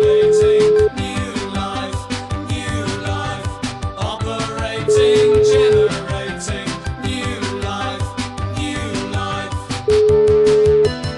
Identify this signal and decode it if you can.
Ringback tone (Europe, 425 Hz), with noise.